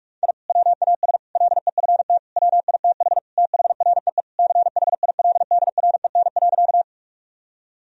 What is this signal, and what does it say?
Morse code, heard as a keyed tone, about 46 wpm, 700 Hz. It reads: I WAS LEFT WITH THREE CHILDREN.
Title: Sentences from Top 1000 Words - Review Part 1 - 45wpm